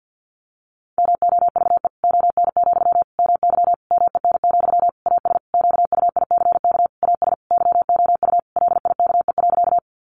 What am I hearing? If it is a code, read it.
Morse code, heard as a keyed tone, about 50 wpm, 700 Hz. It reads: MOVE ON, MY MEN, AS QUICK AS YOU LIKE.
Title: Sentences from Top 500 Words 50wpm